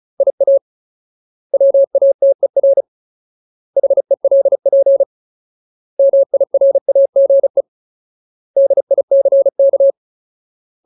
Morse code, heard as a keyed tone, about 35 wpm, 550 Hz. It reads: IA WATER HELP MIRAGE DICK
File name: Ultimate Word List / 35WPM Ultimate